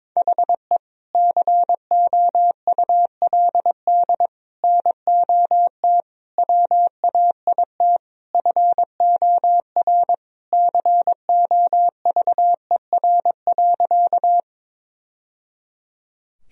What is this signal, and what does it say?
Morse code, heard as a keyed tone, about 22 wpm, 700 Hz. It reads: HE COULD NOT WAIT FOR COVER.
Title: Sentences from Top 1000 Words - Review Part 1 - 22wpm